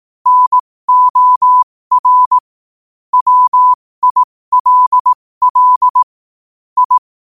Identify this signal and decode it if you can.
Morse code, heard as a keyed tone, about 18 wpm, spaced as for 15 wpm, 1000 Hz. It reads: NOR WILL I